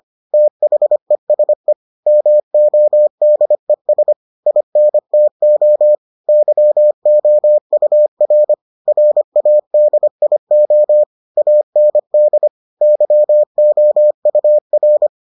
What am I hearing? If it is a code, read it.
Morse code, heard as a keyed tone, about 25 wpm, 600 Hz. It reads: THESE MODES INTO YOUR RADIO AND YOUR